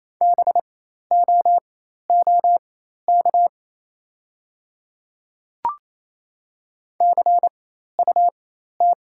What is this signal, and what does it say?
Morse code, heard as a keyed tone, about 27 wpm, spaced as for 13 wpm, 700 Hz. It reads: BOOK CUT
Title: 4x Character Spacing - Top 500 Words - 28wpm